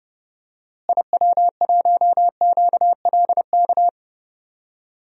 Morse code, heard as a keyed tone, about 30 wpm, 700 Hz. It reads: IW1QLK
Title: DXCC-100 Call Signs - 6 Characters - Repeated 2x - 30wpm